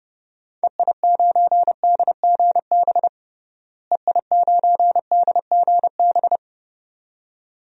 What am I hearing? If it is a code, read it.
Morse code, heard as a keyed tone, about 30 wpm, 700 Hz. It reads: EI9DGB EI9DGB
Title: DXCC-100 Call Signs - 6 Characters - Repeated 2x - 30wpm